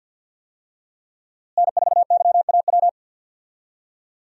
Morse code, heard as a keyed tone, about 48 wpm, 700 Hz. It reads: N4YAW
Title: USA Call Signs - 5 Characters - Repeated 3x - 50wpm